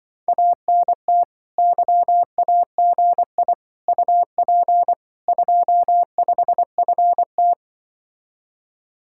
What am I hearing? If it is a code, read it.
Morse code, heard as a keyed tone, about 24 wpm, 700 Hz. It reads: ANT YAGI UP 25FT
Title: Speed-Racing Common QSO Phrases - 20wpm